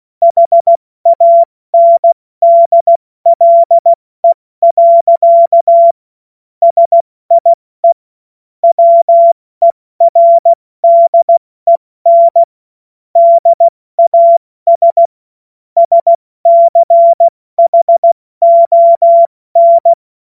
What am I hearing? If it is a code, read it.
Morse code, heard as a keyed tone, about 16 wpm, 680 Hz. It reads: HANDLE. SIE WERDEN DAS SCHON
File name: Morsekurs / Witze / Witz 20. Juni 2023 70 F80